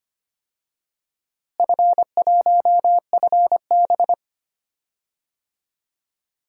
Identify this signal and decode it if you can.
Morse code, heard as a keyed tone, about 25 wpm, 700 Hz. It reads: F1FB
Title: DXCC-100 Call Signs - 4 Characters - Repeated 3x - 25wpm